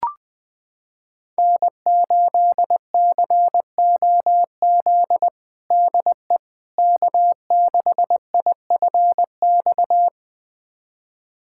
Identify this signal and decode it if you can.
Morse code, heard as a keyed tone, about 20 wpm, 700 Hz. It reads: N8COZ DE K6IFX
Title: USA Callsign DE Callsign - 5 Characters - 20wpm